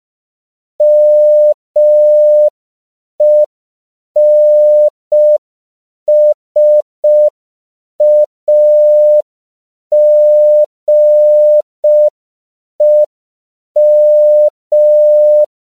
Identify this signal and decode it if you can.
Morse code, heard as a keyed tone, about 5 wpm, 600 Hz. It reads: MENSAGEM